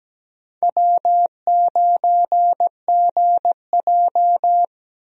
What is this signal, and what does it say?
Morse code, heard as a keyed tone, about 17 wpm, 700 Hz. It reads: W9GJ
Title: USA Call Signs - 4 Characters - Repeated 3x - 17wpm